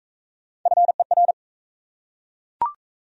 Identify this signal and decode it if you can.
Morse code, heard as a keyed tone, about 39 wpm, 700 Hz. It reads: FER